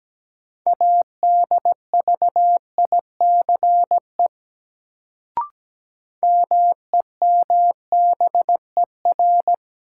Morse code, heard as a keyed tone, about 17 wpm, 700 Hz. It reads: ADVICE MEMBER
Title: Sentences from Top 1100 Words - New Word Drill - 17wpm